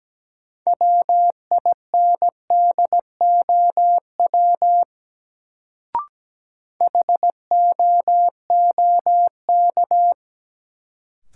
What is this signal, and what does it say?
Morse code, heard as a keyed tone, about 17 wpm, 700 Hz. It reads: WINDOW HOOK
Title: Sentences from Top 1600 Words - New Word Drill - 17wpm